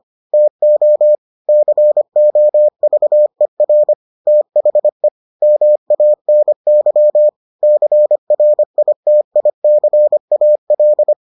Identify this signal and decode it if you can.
Morse code, heard as a keyed tone, about 25 wpm, 600 Hz. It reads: TO COVER THE MANY CRITICAL